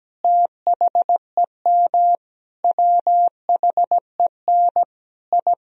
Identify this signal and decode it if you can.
Morse code, heard as a keyed tone, about 17 wpm, 700 Hz. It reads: THEM WHEN I
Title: Sentences from Top 1000 Words - Review Part 2 - 17wpm